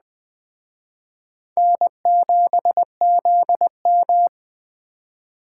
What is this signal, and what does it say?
Morse code, heard as a keyed tone, about 20 wpm, 700 Hz. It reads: N7ZM